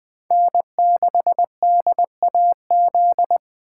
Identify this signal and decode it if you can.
Morse code, heard as a keyed tone, about 20 wpm, 700 Hz. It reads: N6DAZ
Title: Speed Racing - USA 1x3 Call Signs 17wpm